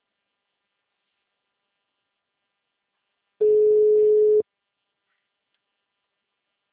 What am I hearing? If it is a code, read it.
Ringback tone (Europe, 425 Hz).
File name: out- -2011- - - .200